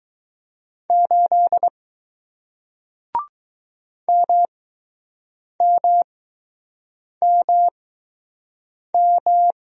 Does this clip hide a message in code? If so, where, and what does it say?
Yes: Morse code from 0.9 to 9.5 s, heard as a keyed tone, about 21 wpm, spaced as for 6 wpm, 700 Hz. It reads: 8 MMMM